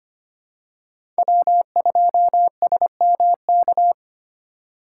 Morse code, heard as a keyed tone, about 25 wpm, 700 Hz. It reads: W2SMK
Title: USA Call Signs - 5 Characters - 4x Character Spacing - 25wpm